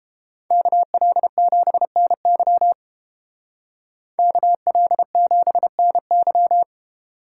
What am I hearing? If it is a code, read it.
Morse code, heard as a keyed tone, about 30 wpm, 700 Hz. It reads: KL7NY KL7NY